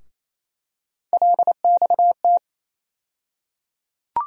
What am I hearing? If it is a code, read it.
Morse code, heard as a keyed tone, about 28 wpm, 700 Hz. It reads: LXT
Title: USA Call Sign Postfixes - 3 Letters 28wpm